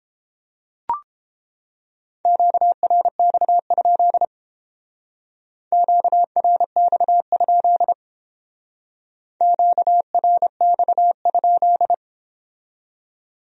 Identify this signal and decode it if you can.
Morse code, heard as a keyed tone, about 31 wpm, 700 Hz. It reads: QRX? QRX? QRX?